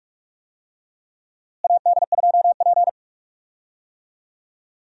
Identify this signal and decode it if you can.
Morse code, heard as a keyed tone, about 46 wpm, 700 Hz. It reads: AD2P
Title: USA Call Signs - 4 Characters - Repeated 3x - 45wpm